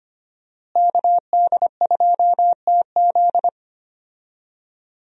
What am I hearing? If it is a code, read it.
Morse code, heard as a keyed tone, about 25 wpm, 700 Hz. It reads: KD2TZ
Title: USA Call Signs - 5 Characters - Repeated 3x - 25wpm